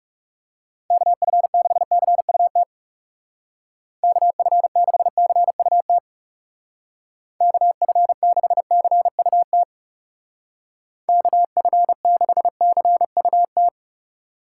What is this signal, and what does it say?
Morse code, heard as a keyed tone, about 35 wpm, 700 Hz. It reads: KF6CUT KF6CUT KF6CUT KF6CUT
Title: Speed Racing - USA 2x3 Call Signs 30wpm